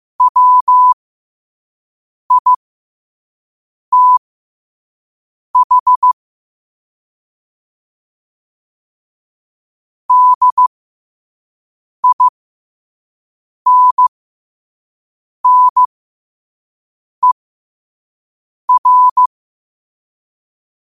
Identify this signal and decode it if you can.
Morse code, heard as a keyed tone, about 15 wpm, spaced as for 5 wpm, 1000 Hz. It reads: WITH DINNER